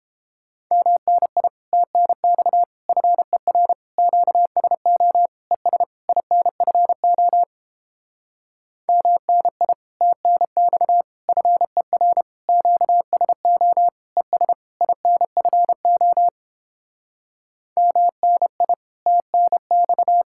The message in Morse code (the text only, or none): MNI TNX FER QSO ES INFO MNI TNX FER QSO ES INFO MNI TNX